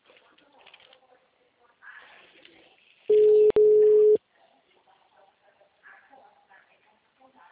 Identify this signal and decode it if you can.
Ringback tone (Europe, 425 Hz), with speech.